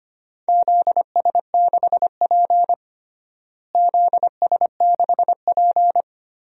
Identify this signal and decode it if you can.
Morse code, heard as a keyed tone, about 25 wpm, 700 Hz. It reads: ZS6P ZS6P
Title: DXCC-100 Call Signs - 4 Characters - Repeated 3x - 25wpm